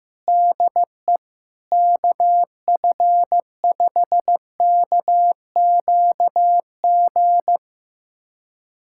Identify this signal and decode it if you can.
Morse code, heard as a keyed tone, about 15 wpm, 700 Hz. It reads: DE KF5KQG
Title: USA Callsign DE Callsign - 6 Characters - 15wpm